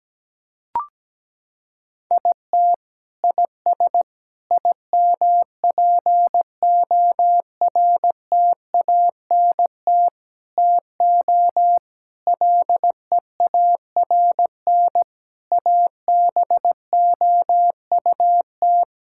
Morse code, heard as a keyed tone, about 17 wpm, 700 Hz. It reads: IT IS IMPORTANT TO LEARN ABOUT
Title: Sentences from Top 1000 Words - Review Part 6 - 17wpm